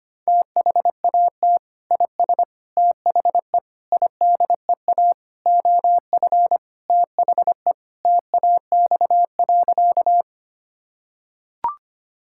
Morse code, heard as a keyed tone, about 25 wpm, 700 Hz. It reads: THAT IS THE IDEA OF THE TAX.
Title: Sentences from Top 1000 Words - Review Part 5 - 25wpm